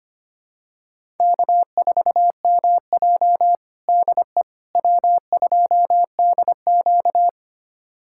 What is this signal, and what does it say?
Morse code, heard as a keyed tone, about 25 wpm, 700 Hz. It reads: K4MJ DE W2DQ
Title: USA Callsign DE Callsign - 4 Characters - 25wpm